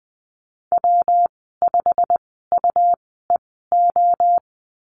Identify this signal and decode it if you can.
Morse code, heard as a keyed tone, about 20 wpm, spaced as for 15 wpm, 700 Hz. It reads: W5UEO
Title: 128 Lesson 21g - Number 6 - Call Signs 20wpm